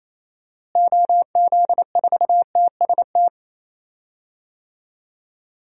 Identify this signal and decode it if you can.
Morse code, heard as a keyed tone, about 27 wpm, 700 Hz. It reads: OZ4TST
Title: DXCC-100 Call Signs - 6 Characters - Repeated 3x - 28wpm